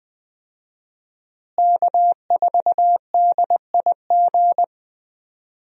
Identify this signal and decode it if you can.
Morse code, heard as a keyed tone, about 20 wpm, 700 Hz. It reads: K4DIG
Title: USA Call Signs - 5 Characters - Repeated 2x - 20wpm